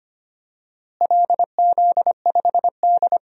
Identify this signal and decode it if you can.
Morse code, heard as a keyed tone, about 25 wpm, 700 Hz. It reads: LZ5D
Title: DXCC-100 Call Signs - 4 Characters - Repeated 2x - 25wpm